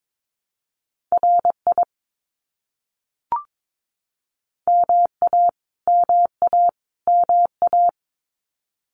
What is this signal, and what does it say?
Morse code, heard as a keyed tone, about 22 wpm, 700 Hz. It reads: RI MA MA MA
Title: US States Repeated 3x 22wpm